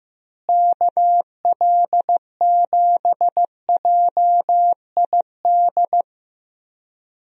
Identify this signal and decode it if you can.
Morse code, heard as a keyed tone, about 15 wpm, 700 Hz. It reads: KL7JID
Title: DXCC-100 Call Signs - 6 Characters - Repeated 2x - 15wpm